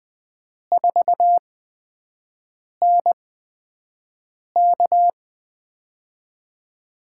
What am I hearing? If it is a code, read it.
Morse code, heard as a keyed tone, about 20 wpm, spaced as for 5 wpm, 700 Hz. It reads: 4NK
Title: USA Call Signs - 5 Characters - 8x Character Spacing - 20wpm